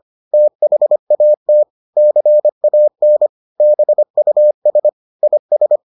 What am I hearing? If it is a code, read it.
Morse code, heard as a keyed tone, about 25 wpm, 600 Hz. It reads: THAT CAN BUS IS